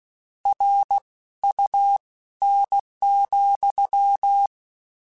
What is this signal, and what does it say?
Morse code, heard as a keyed tone, about 16 wpm, spaced as for 13 wpm, 800 Hz. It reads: RUN,
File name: Misc / Misc-mp3s / WordPairs sp @15